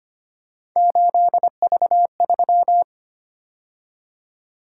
Morse code, heard as a keyed tone, about 25 wpm, 700 Hz. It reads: 8V3
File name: Three Letter-Number Combinations - 25wpm